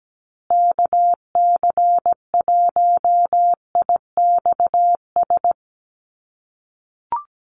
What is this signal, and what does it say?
Morse code, heard as a keyed tone, about 17 wpm, 700 Hz. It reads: KC1IXS
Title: USA Call Signs Repeated 3x 17wpm